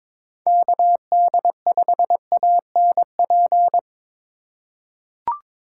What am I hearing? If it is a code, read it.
Morse code, heard as a keyed tone, about 22 wpm, 700 Hz. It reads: KD5ANP
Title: USA Call Signs - 6 Characters - 4x Character Spacing - 22wpm